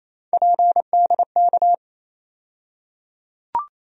Morse code, heard as a keyed tone, about 28 wpm, 700 Hz. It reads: PDK
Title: USA Call Sign Postfixes - 3 Letters 28wpm